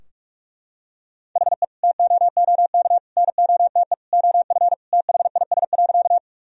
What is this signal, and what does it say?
Morse code, heard as a keyed tone, about 42 wpm, 700 Hz. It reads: HE TOOK NOTE OF THIS.
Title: Sentences from Top 1000 Words - Review Part 2 - 45wpm